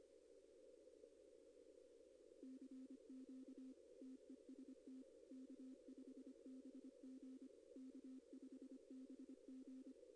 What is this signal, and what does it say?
Morse code, heard as a keyed tone, about 24 wpm, 260 Hz. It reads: CQ TEST K5BG K5BG
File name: contests / arrl-160-k5tr-op-w5tn -2200